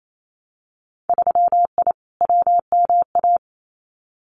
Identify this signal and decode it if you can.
Morse code, heard as a keyed tone, about 28 wpm, 700 Hz. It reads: 3I WMA
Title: Winter Field Day Exchanges 28wpm